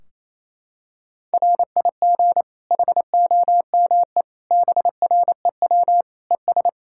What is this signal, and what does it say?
Morse code, heard as a keyed tone, about 28 wpm, 700 Hz. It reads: RIG HOME BREW ES
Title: Practice QSO - Exchange 2 - Info - 28wpm